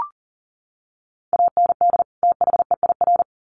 Morse code, heard as a keyed tone, about 40 wpm, 700 Hz. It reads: AND THEIR
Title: Sets of 2 Words 40wpm